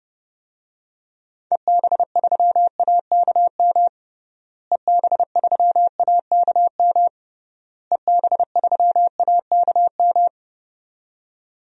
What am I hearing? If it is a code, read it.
Morse code, heard as a keyed tone, about 30 wpm, 700 Hz. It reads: EB3AKM EB3AKM EB3AKM
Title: DXCC-100 Call Signs - 6 Characters - Repeated 3x - 30wpm